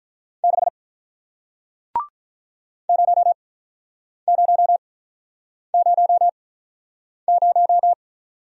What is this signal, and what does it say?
Morse code, heard as a keyed tone, about 48 wpm, spaced as for 9 wpm, 700 Hz. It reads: 6 0000